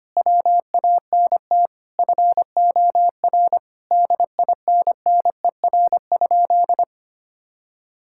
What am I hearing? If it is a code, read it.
Morse code, heard as a keyed tone, about 25 wpm, 700 Hz. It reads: WANT FOR DINNER?
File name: practice / 25wpm / Sentences with 4-Word Phrases - 25wpm